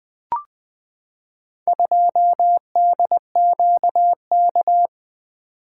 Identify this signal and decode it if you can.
Morse code, heard as a keyed tone, about 20 wpm, 700 Hz. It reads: 2DQK
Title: Four Letter-Number Combinations - 20wpm